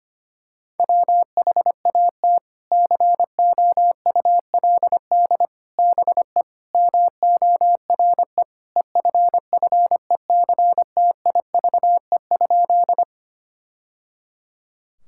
Morse code, heard as a keyed tone, about 25 wpm, 700 Hz. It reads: WHAT COULD BE MORE EFFECTIVE?
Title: Sentences from Top 1000 Words - Review Part 6 - 25wpm